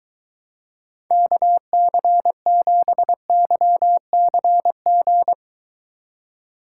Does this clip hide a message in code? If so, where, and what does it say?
Yes: Morse code from 1.1 to 5.3 s, heard as a keyed tone, about 23 wpm, 700 Hz. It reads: KC7YCG